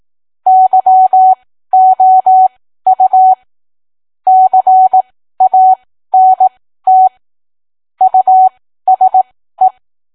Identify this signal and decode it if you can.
Morse code, heard as a keyed tone, about 18 wpm, spaced as for 13 wpm, 750 Hz. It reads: YOU CANT USE